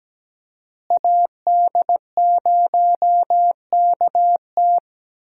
Morse code, heard as a keyed tone, about 17 wpm, 700 Hz. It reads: AD0KT
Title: USA Call Signs - 5 Characters - Repeated 2x - 17wpm